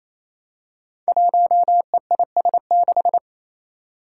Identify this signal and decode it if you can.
Morse code, heard as a keyed tone, about 27 wpm, 700 Hz. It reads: 1EIS6
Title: Five Challenging Letter-Number Combinations 28wpm